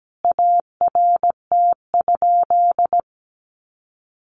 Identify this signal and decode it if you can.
Morse code, heard as a keyed tone, about 17 wpm, 700 Hz. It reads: ART?
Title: Sentences from Top 1400 Words 17wpm